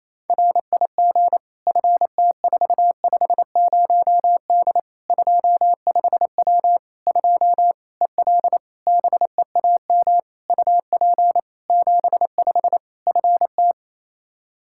Morse code, heard as a keyed tone, about 28 wpm, 700 Hz. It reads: RIG FT450D 25W 2 EL BEAM UP 75 FT
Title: Practice QSO - Exchange 2 - Info - 28wpm